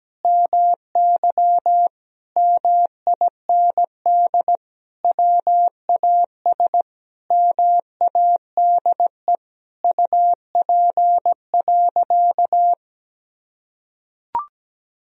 Morse code, heard as a keyed tone, about 17 wpm, 700 Hz. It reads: MY MIND WAS MADE UP.
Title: Sentences from Top 1000 Words - Review Part 2 - 17wpm